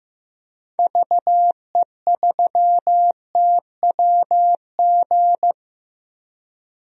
Morse code, heard as a keyed tone, about 15 wpm, 700 Hz. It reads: VE3TWG
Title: DXCC-100 Call Signs - 6 Characters - Repeated 3x - 15wpm